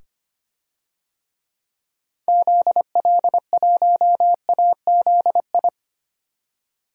Morse code, heard as a keyed tone, about 25 wpm, 700 Hz. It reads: ZL1AZI
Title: DXCC-100 Call Signs - 6 Characters - 25wpm